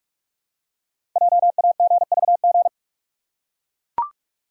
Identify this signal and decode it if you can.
Morse code, heard as a keyed tone, about 44 wpm, 700 Hz. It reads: JAGVG
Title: Code Groups - 5 Letters - 45wpm